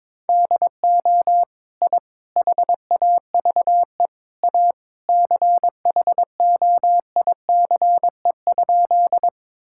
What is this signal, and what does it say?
Morse code, heard as a keyed tone, about 22 wpm, 700 Hz. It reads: DO I HAVE A CHOICE?
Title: Sentences from Top 1000 Words - Review Part 6 - 22wpm